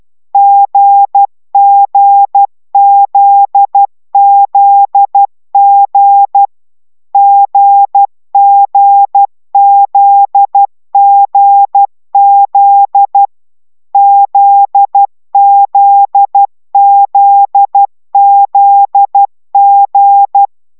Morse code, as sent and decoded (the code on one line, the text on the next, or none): --. --. --.. --.. --. / --. --. --.. --. --.. / --.. --.. --.. --.. --.
GGZZG GGZGZ ZZZZG